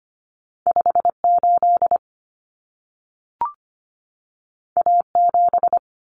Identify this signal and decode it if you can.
Morse code, heard as a keyed tone, about 25 wpm, 700 Hz. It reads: 58 A7
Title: Common Cut Numbers - 2-Digit 25wpm